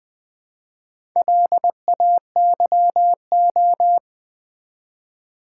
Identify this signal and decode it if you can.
Morse code, heard as a keyed tone, about 20 wpm, 690 Hz. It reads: LAYO
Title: Build A Word - 2 Letters At A Time - 6-Letter Words - 20wpm